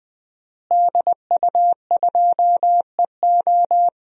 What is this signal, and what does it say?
Morse code, heard as a keyed tone, about 20 wpm, 700 Hz. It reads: DU2EO
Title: DXCC-100 Call Signs - 5 Characters - Repeated 3x - 20wpm